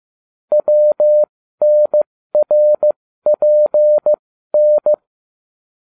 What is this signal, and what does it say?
Morse code, heard as a keyed tone, about 15 wpm, spaced as for 13 wpm, 600 Hz. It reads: WNRPN